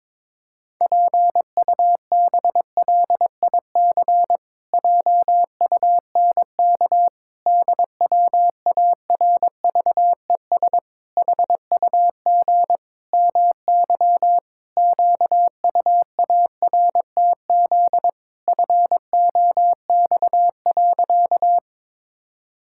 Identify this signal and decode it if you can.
Morse code, heard as a keyed tone, about 22 wpm, 700 Hz. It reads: PUBLIC JUNK DWARVES HUG MY QUARTZ FOX.
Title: Pangram Challenge v2 - 22wpm